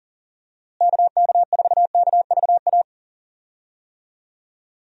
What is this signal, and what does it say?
Morse code, heard as a keyed tone, about 37 wpm, 700 Hz. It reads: KK4KVA
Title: USA Call Signs - 6 Characters - Repeated 2x - 40wpm